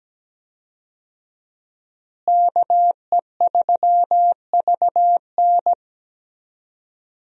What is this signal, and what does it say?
Morse code, heard as a keyed tone, about 17 wpm, 700 Hz. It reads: KE3VN
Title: USA Call Signs - 5 Characters - Repeated 3x - 17wpm